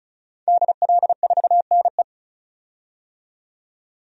Morse code, heard as a keyed tone, about 36 wpm, 700 Hz. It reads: DL4NE
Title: DXCC-100 Call Signs - 5 Characters - 35wpm